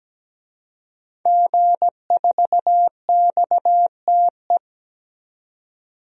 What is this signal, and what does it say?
Morse code, heard as a keyed tone, about 17 wpm, 700 Hz. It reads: G4XTE
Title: DXCC-100 Call Signs - 5 Characters - 17wpm